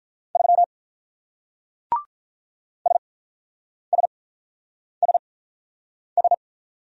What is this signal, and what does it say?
Morse code, heard as a keyed tone, about 45 wpm, spaced as for 9 wpm, 700 Hz. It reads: 3 SSSS